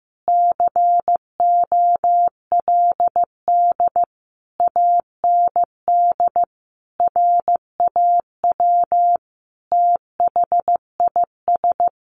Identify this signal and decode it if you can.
Morse code, heard as a keyed tone, about 15 wpm, 700 Hz. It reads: COLD AND RAW THIS
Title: Sentences from Top 1500 Words 15wpm